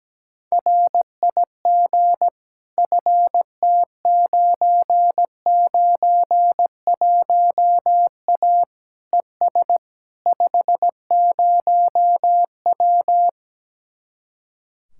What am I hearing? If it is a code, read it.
Morse code, heard as a keyed tone, about 17 wpm, 700 Hz. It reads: RIG FT991A ES 50W
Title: Speed-Racing Common QSO Phrases - 17wpm